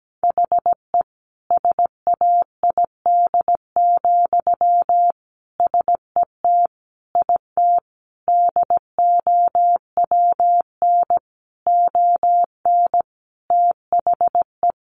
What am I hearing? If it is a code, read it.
Morse code, heard as a keyed tone, about 17 wpm, 700 Hz. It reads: HE SAID, SET IT DOWN ON THE
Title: Sentences from Top 1400 Words 17wpm